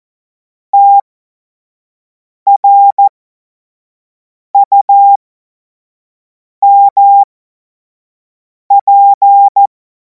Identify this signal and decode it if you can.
Morse code, heard as a keyed tone, about 14 wpm, spaced as for 5 wpm, 800 Hz. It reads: TRUMP
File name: Morsekurs / Witze / Witz 02. Juni 2026 25 F70